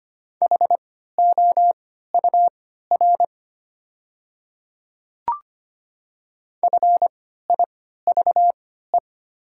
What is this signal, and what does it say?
Morse code, heard as a keyed tone, about 25 wpm, spaced as for 14 wpm, 700 Hz. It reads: HOUR FIVE
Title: 3x Character Spacing - Top 500 Words - 25wpm